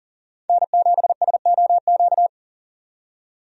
Morse code, heard as a keyed tone, about 40 wpm, 700 Hz. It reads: N7SOQ
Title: USA Call Signs - 5 Characters - 40wpm